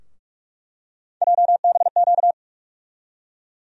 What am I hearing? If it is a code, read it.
Morse code, heard as a keyed tone, about 42 wpm, 700 Hz. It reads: JBQ
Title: USA Call Sign Postfixes - 3 Letters 45wpm